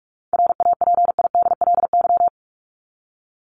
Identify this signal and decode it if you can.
Morse code, heard as a keyed tone, about 44 wpm, 700 Hz. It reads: RAPIDLY
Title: LY Suffix - Top 100 Words with Base Words 45wpm